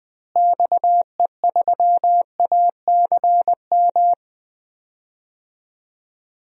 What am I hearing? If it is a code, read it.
Morse code, heard as a keyed tone, about 20 wpm, 700 Hz. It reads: XE3ACM
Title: DXCC-100 Call Signs - 6 Characters - Repeated 3x - 20wpm